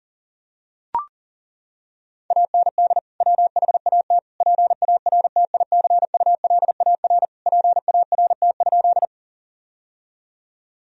Morse code, heard as a keyed tone, about 40 wpm, 700 Hz. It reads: AND WHAT PARTICULAR PART?
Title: Sentences from Top 1000 Words - Review Part 4 - 40wpm